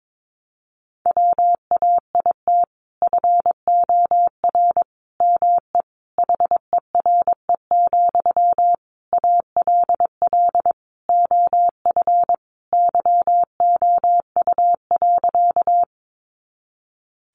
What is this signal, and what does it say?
Morse code, heard as a keyed tone, about 22 wpm, 700 Hz. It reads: WAIT FOR ME HERE, ALL OF YOU.
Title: Sentences from Top 500 Words 22wpm